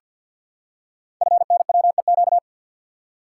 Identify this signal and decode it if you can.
Morse code, heard as a keyed tone, about 48 wpm, 700 Hz. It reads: FNWEQ
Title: Code Groups - 5 Letters - 50wpm